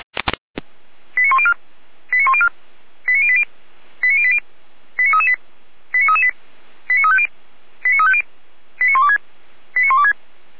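Five-tone selective call (ZVEI), heard as quick runs of five tones, tones of about 70 ms. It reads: 89104 89104 89080 89080 89338 89338 89350 89350 89126 89126
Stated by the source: ZVEI five-tone selective call 89280, received on an SDR.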